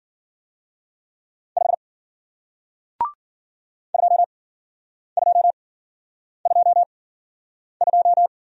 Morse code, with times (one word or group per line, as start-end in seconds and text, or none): 1.6-1.7 5
3.9-8.3 2222